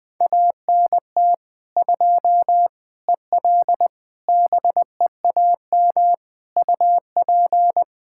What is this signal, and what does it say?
Morse code, heard as a keyed tone, about 21 wpm, 700 Hz. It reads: ANT 2 EL BEAM UP
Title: Practice QSO - Exchange 2 - Info - 20wpm